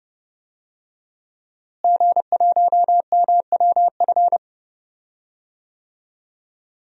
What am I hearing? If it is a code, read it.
Morse code, heard as a keyed tone, about 30 wpm, 700 Hz. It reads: G1MWF
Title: DXCC-100 Call Signs - 5 Characters - Repeated 3x - 30wpm